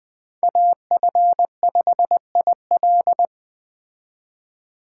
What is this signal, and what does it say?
Morse code, heard as a keyed tone, about 20 wpm, 700 Hz. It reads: AF5IL